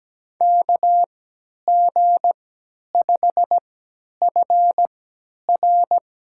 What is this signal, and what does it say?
Morse code, heard as a keyed tone, about 17 wpm, spaced as for 10 wpm, 700 Hz. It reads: KG5FR
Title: USA Call Signs - 3x Character Spacing - 17wpm